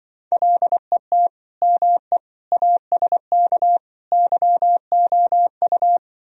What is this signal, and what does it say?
Morse code, heard as a keyed tone, about 24 wpm, 700 Hz. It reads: LET ME ASK YOU
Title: Speed-Racing - Sets of 4 Words 20wpm